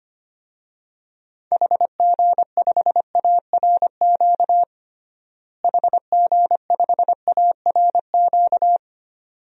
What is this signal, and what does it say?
Morse code, heard as a keyed tone, about 25 wpm, 700 Hz. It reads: HG5ARQ HG5ARQ
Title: DXCC-100 Call Signs - 6 Characters - Repeated 2x - 25wpm